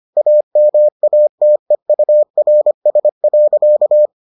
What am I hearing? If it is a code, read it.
Morse code, heard as a keyed tone, about 25 wpm, 600 Hz. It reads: AMATEURS.